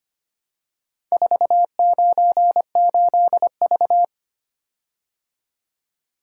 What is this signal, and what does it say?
Morse code, heard as a keyed tone, about 25 wpm, 700 Hz. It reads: 498V
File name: practice / 25wpm / Four Letter-Number Combinations - 25wpm